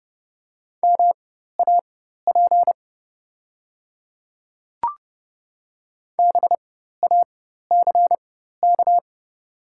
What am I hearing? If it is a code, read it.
Morse code, heard as a keyed tone, about 30 wpm, spaced as for 14 wpm, 700 Hz. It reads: MAP BACK
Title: 4x Character Spacing - Top 500 Words - 30wpm